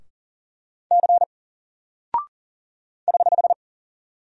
Morse code, heard as a keyed tone, about 39 wpm, 700 Hz. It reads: C <HH>
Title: Single Character (Letter:Number:Punctuation:Prosign) Alt 40wpm